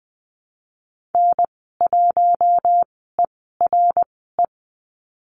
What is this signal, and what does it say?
Morse code, heard as a keyed tone, about 20 wpm, spaced as for 15 wpm, 700 Hz. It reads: N1ERE
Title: 055 Lesson 10g - Letter r - Call Signs 20wpm